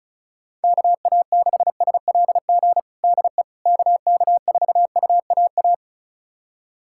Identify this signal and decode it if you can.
Morse code, heard as a keyed tone, about 36 wpm, 700 Hz. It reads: KA6SLG DE KK4UAA
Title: USA Callsign DE Callsign - 6 Characters - 35wpm